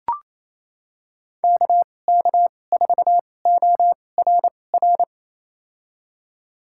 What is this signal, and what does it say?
Morse code, heard as a keyed tone, about 28 wpm, spaced as for 20 wpm, 700 Hz. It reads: KK4ORR
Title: USA Call Signs - 6 Characters - 2x Character Spacing - 28wpm